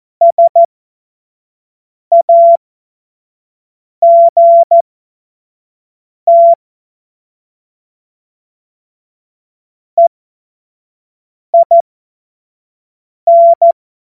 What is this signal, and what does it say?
Morse code, heard as a keyed tone, about 14 wpm, spaced as for 5 wpm, 680 Hz. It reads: SAGT EIN
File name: Morsekurs / Witze / Witz 04. Januar 2022, 25 F70